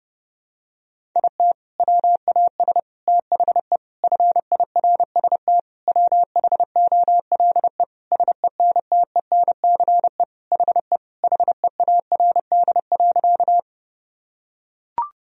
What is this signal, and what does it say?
Morse code, heard as a keyed tone, about 32 wpm, 700 Hz. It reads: IT WAS THE FIRST WHOLE SENTENCE HE HEARD.
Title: Sentences from Top 1000 Words - Review Part 1 - 30wpm